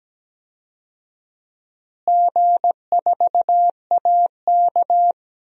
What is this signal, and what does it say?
Morse code, heard as a keyed tone, about 17 wpm, 700 Hz. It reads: G4AK
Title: DXCC-100 Call Signs - 4 Characters - Repeated 2x - 17wpm